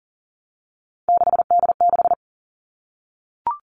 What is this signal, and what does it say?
Morse code, heard as a keyed tone, about 40 wpm, 700 Hz. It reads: 6D6
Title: All Cut Numbers - 3-Digit 40wpm